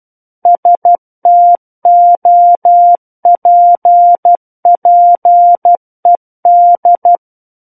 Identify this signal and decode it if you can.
Morse code, heard as a keyed tone, about 12 wpm, 700 Hz. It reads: STOPPED
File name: Book One / 03-Chapter 3 - 12 WPM